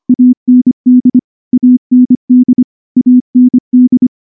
Morse code, heard as a keyed tone, about 24 wpm, 250 Hz. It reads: AND AND AND
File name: morse / morse-training-25WPM